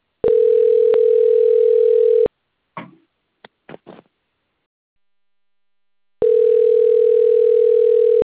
Ringback tone (North America), 440 + 480 Hz.
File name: rg-1600-unknown